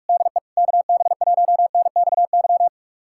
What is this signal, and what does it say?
Morse code, heard as a keyed tone, about 42 wpm, 700 Hz. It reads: DE KB1NXY
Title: USA Callsign DE Callsign - 6 Characters - 45wpm